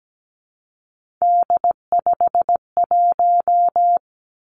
Morse code, heard as a keyed tone, about 17 wpm, 700 Hz. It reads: D51
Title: Three Challenging Letter-Number Combinations 17wpm